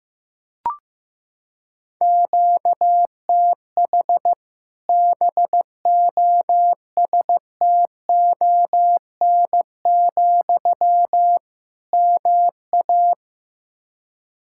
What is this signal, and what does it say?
Morse code, heard as a keyed tone, about 16 wpm, 700 Hz. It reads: QTH BOSTON, MA
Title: Practice QSO - Exchange 1 - Beginning - 15wpm